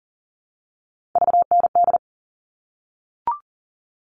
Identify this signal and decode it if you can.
Morse code, heard as a keyed tone, about 40 wpm, 700 Hz. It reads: VND
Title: All Cut Numbers - 3-Digit 40wpm